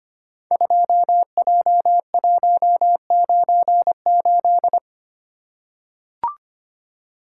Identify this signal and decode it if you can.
Morse code, heard as a keyed tone, about 25 wpm, 700 Hz. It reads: 2J198
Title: Code Groups - 5 Letter:Numbers - 25wpm